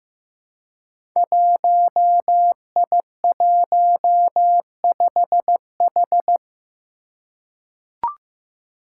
Morse code, heard as a keyed tone, about 15 wpm, 700 Hz. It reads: 1I15H
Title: Five Challenging Letter-Number Combinations 15wpm